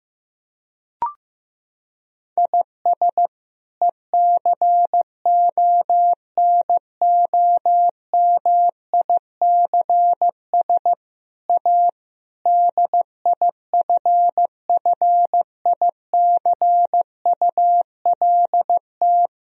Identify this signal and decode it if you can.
Morse code, heard as a keyed tone, about 15 wpm, 700 Hz. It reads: IS ECONOMICS A DIFFICULT
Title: Sentences from Top 1000 Words - Review Part 3 - 15wpm